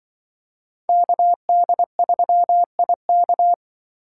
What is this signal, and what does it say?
Morse code, heard as a keyed tone, about 24 wpm, 700 Hz. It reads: KD3IK
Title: Speed Racing - USA 2x2 Call Signs 20wpm